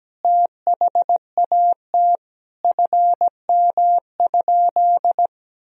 Morse code, heard as a keyed tone, about 17 wpm, 700 Hz. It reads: THAT FM?
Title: Sentences from Top 300 Words and Abbreviations 17wpm